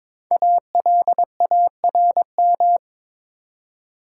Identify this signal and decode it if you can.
Morse code, heard as a keyed tone, about 23 wpm, 700 Hz. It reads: ALARM
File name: 5 Letter Words - First 3 Letters Given - 22wpm